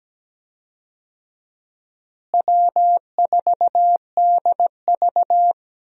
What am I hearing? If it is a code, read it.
Morse code, heard as a keyed tone, about 17 wpm, 700 Hz. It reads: W4DV
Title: USA Call Signs - 4 Characters - Repeated 2x - 17wpm